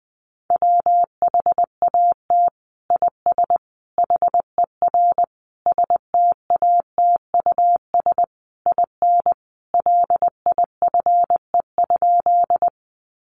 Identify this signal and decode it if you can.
Morse code, heard as a keyed tone, about 20 wpm, 700 Hz. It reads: WHAT IS HER STATUS IN LIFE?
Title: Sentences from Top 1400 Words 20wpm